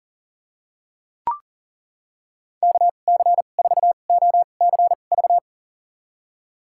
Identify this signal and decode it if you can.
Morse code, heard as a keyed tone, about 40 wpm, spaced as for 29 wpm, 700 Hz. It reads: KC4OCV
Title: USA Call Signs - 6 Characters - 2x Character Spacing - 40wpm